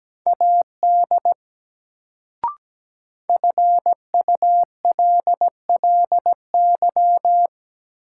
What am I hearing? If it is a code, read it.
Morse code, heard as a keyed tone, about 17 wpm, 700 Hz. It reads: AD FULLY
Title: Sentences from Top 1300 Words - New Word Drill Alt - 17wpm